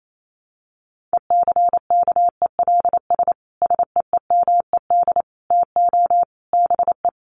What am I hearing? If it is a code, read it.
Morse code, heard as a keyed tone, about 28 wpm, 700 Hz. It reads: ECKELS SEEMED TO BE
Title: A Sound Of Thunder 28wpm